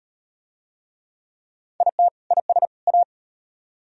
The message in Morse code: IT IS A

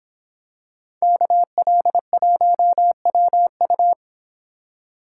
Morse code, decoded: KL1WU

